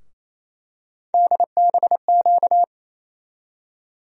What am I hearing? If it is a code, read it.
Morse code, heard as a keyed tone, about 28 wpm, 700 Hz. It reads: DBQ
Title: USA Call Sign Postfixes - 3 Letters 28wpm